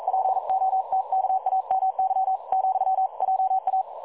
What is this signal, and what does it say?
Morse code, heard as a keyed tone, about 42 wpm, 760 Hz. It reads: POTKAMY, JA
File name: dlazaawansowanych / nagraniazpasma / websdr recording 3565.0kHz